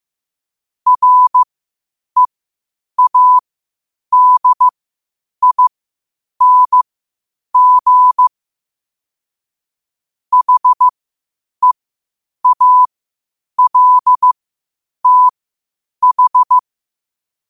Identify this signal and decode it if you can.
Morse code, heard as a keyed tone, about 15 wpm, spaced as for 8 wpm, 1000 Hz. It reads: READING HEALTH
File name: qotdfiles / 8wpm